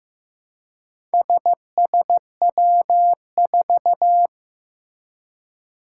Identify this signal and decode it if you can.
Morse code, heard as a keyed tone, about 15 wpm, 700 Hz. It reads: SSW4
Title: Four Challenging Letter-Number Combinations 15wpm